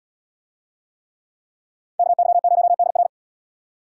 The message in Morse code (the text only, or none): KY0DR